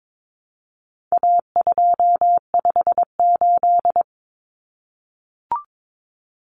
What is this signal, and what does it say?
Morse code, heard as a keyed tone, about 22 wpm, 700 Hz. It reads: A258
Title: Common Cut Numbers - 4-Digit 22wpm